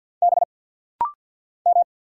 Morse code, heard as a keyed tone, about 49 wpm, 690 Hz. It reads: B M
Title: Single Letters - Rapid-Fire 50wpm